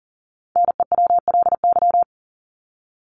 Morse code, heard as a keyed tone, about 40 wpm, 700 Hz. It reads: NEWLY
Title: LY Suffix - Top 100 Words with Base Words 40wpm